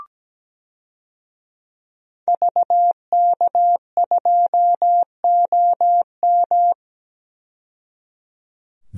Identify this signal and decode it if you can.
Morse code, heard as a keyed tone, about 17 wpm, 700 Hz. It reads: VK2OM
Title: DXCC-100 Call Signs - 5 Characters - 17wpm